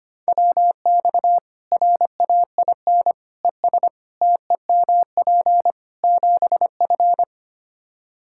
Morse code, heard as a keyed tone, about 25 wpm, 700 Hz. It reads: WX RAIN ES TEMP 7F
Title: Practice QSO - Exchange 2 - Info - 25wpm